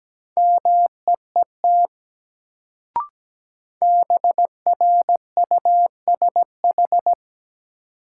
Morse code, heard as a keyed tone, about 17 wpm, 700 Hz. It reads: MEET BRUSH
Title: Sentences from Top 1100 Words - New Word Drill Alt - 17wpm